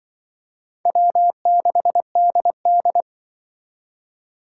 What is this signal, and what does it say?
Morse code, heard as a keyed tone, about 23 wpm, 700 Hz. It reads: W6DD
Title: Speed Racing - USA 1x2 Call Signs 20wpm